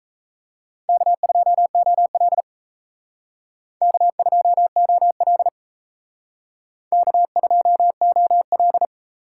Morse code, heard as a keyed tone, about 35 wpm, 700 Hz. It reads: K2OL K2OL K2OL
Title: Speed Racing - USA 1x2 Call Signs 28wpm